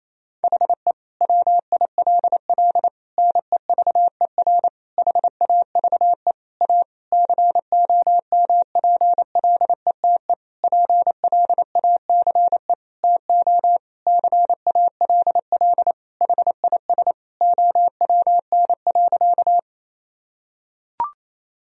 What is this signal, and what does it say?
Morse code, heard as a keyed tone, about 28 wpm, 700 Hz. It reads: HE WILL NEVER HAVE A COMPLETE PLACE TO CALL HIS OWN.
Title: Sentences from Top 1000 Words - Review Part 2 - 28wpm